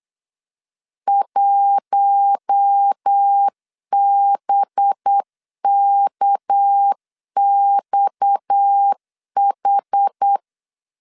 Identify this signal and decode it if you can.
Morse code, heard as a keyed tone, about 8 wpm, 800 Hz. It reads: 1BKXH
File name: LEZIONI / 6 PKX0 / 6 PKX 8wpm 3041